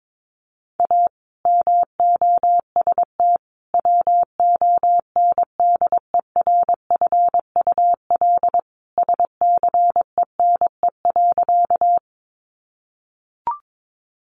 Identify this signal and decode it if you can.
Morse code, heard as a keyed tone, about 22 wpm, 700 Hz. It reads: A MOST WONDERFUL SCENE.
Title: Sentences from Top 1100 Words 22wpm